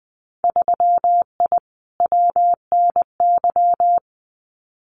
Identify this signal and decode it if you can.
Morse code, heard as a keyed tone, about 20 wpm, 700 Hz. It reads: 3I WNY
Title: Winter Field Day Exchanges 20wpm